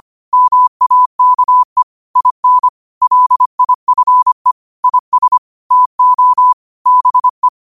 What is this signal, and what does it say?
Morse code, heard as a keyed tone, about 25 wpm, 1000 Hz. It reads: MAKE IN LIFE IS TO BE
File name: qotdfiles / 25wpm